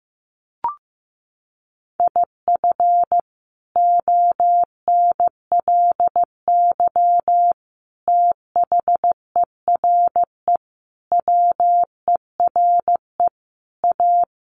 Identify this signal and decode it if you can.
Morse code, heard as a keyed tone, about 15 wpm, 700 Hz. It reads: IF ONLY THERE WERE A
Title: Sentences from Top 1400 Words 15wpm